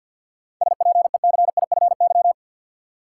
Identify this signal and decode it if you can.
Morse code, heard as a keyed tone, about 51 wpm, 700 Hz. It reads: SPECIFY
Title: Seven Letter Words - 50wpm